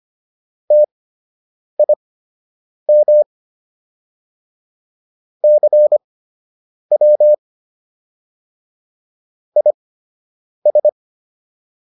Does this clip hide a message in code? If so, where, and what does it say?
Yes: Morse code from 0.7 to 10.9 s, heard as a keyed tone, about 25 wpm, spaced as for 8 wpm, 600 Hz. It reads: TIM CW IS